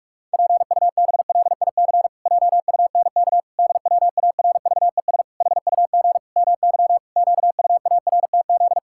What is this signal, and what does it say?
Morse code, heard as a keyed tone, about 46 wpm, 700 Hz. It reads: PUBLIC JUNK DWARVES HUG MY QUARTZ
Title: Pangram Challenge v2 - 45wpm